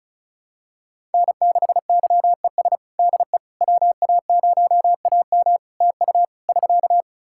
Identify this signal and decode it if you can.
Morse code, heard as a keyed tone, about 36 wpm, 700 Hz. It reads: N6YES DE WA0AM TU <SK>
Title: Practice QSO - Exchange 4 - Ending - 35wpm